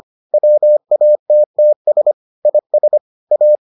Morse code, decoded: WATTS IS A

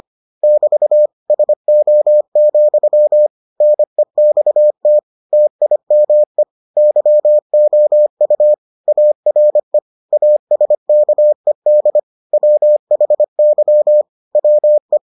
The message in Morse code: <BT> SO, NEXT TIME YOU ARE ASKED WHY WE

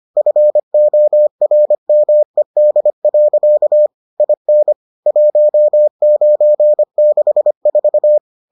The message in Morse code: FORMED. IN 1964

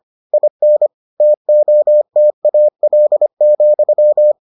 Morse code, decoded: IN TOTAL,

